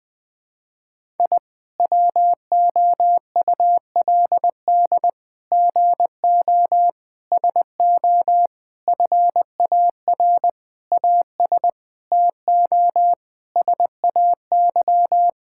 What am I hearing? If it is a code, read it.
Morse code, heard as a keyed tone, about 20 wpm, 700 Hz. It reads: I WOULD GO SO FAR AS TO SAY
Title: Sets of 9 Words - 20wpm